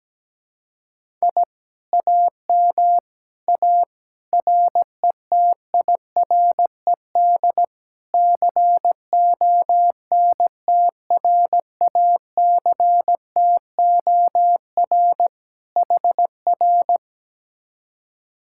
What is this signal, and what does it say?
Morse code, heard as a keyed tone, about 17 wpm, 700 Hz. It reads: I AM A RETIRED CONTRACTOR HR
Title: Speed-Racing Common QSO Phrases - 17wpm